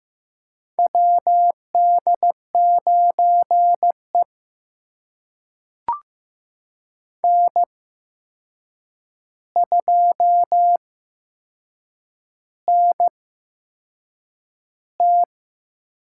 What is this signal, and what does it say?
Morse code, heard as a keyed tone, about 15 wpm, spaced as for 6 wpm, 700 Hz. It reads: WD9E N2NT
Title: USA Call Signs - 4 Characters - 8x Character Spacing - 15wpm